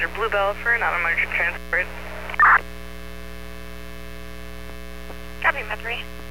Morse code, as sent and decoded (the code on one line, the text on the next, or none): none